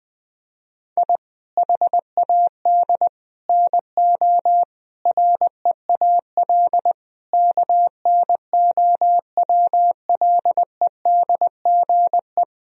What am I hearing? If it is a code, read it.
Morse code, heard as a keyed tone, about 20 wpm, 700 Hz. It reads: I HAD NO REAL KNOWLEDGE
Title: Sentences from Top 1000 Words - Review Part 3 - 20wpm